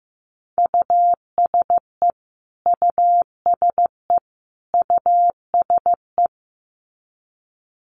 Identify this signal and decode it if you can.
Morse code, heard as a keyed tone, about 15 wpm, 700 Hz. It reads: USE USE USE
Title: Top 200 Words Repeated 15wpm